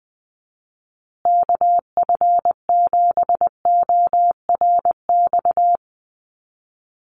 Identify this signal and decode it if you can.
Morse code, heard as a keyed tone, about 20 wpm, 700 Hz. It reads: KF7ORX
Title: USA Call Signs Repeated 3x 20wpm